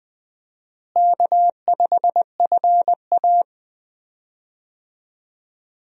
Morse code, heard as a keyed tone, about 20 wpm, 700 Hz. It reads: K5FA
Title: USA Call Signs - 4 Characters - Repeated 2x - 20wpm